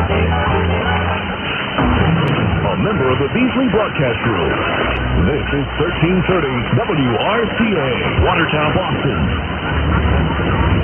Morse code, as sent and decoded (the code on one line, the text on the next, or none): none